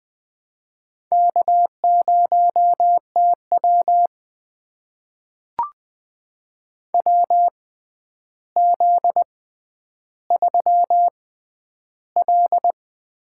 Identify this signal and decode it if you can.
Morse code, heard as a keyed tone, about 20 wpm, spaced as for 10 wpm, 700 Hz. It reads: K0TW WZ3L
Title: USA Call Signs - 4 Characters - 6x Character Spacing - 20wpm